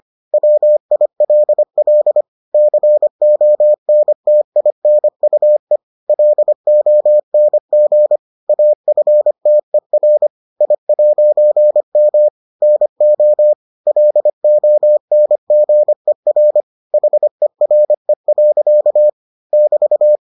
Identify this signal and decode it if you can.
Morse code, heard as a keyed tone, about 25 wpm, 600 Hz. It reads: WILL CONTINUE LONG AFTER I'M NO LONGER HERE. <BT>